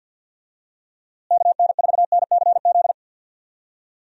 Morse code, heard as a keyed tone, about 50 wpm, 700 Hz. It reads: KN4NCZ
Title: USA Call Signs - 6 Characters - Repeated 3x - 50wpm